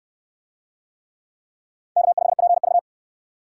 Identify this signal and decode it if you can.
Morse code, heard as a keyed tone, about 63 wpm, 700 Hz. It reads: K5PV